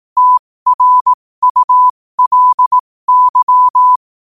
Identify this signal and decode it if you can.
Morse code, heard as a keyed tone, about 18 wpm, spaced as for 15 wpm, 1000 Hz. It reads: TRULY
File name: qotdfiles / 15wpm